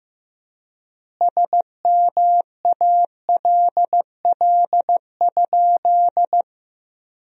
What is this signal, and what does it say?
Morse code, heard as a keyed tone, about 15 wpm, 700 Hz. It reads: SMALL?
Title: 4x Word-Spacing - Sentences from Top 200 Words - 15wpm